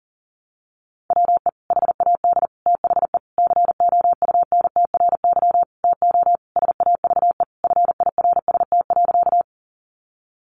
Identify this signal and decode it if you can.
Morse code, heard as a keyed tone, about 40 wpm, 700 Hz. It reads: WE HAD THE COUNTRY TO SAVE FIRST.
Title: Sentences from Top 1100 Words 40wpm